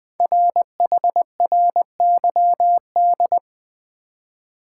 Morse code, heard as a keyed tone, about 20 wpm, 700 Hz. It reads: RHRYD
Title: Code Groups - 5 Letters - 20wpm